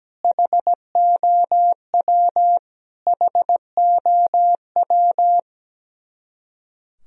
Morse code, heard as a keyed tone, about 17 wpm, 700 Hz. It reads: HOW HOW
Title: Top 100 Words Repeated 17wpm